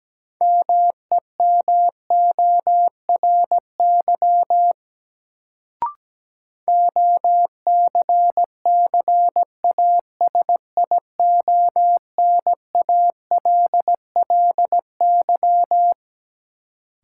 Morse code, heard as a keyed tone, about 17 wpm, 700 Hz. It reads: MEMORY OCCASIONALLY
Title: Sentences from Top 1300 Words - New Word Drill - 17wpm